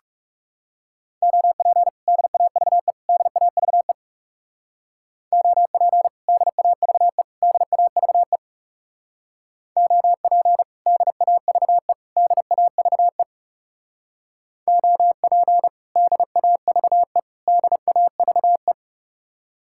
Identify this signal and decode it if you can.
Morse code, heard as a keyed tone, about 37 wpm, 700 Hz. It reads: OP DAVE DAVE OP DAVE DAVE OP DAVE DAVE OP DAVE DAVE
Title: Speed-Racing Common QSO Phrases - 30wpm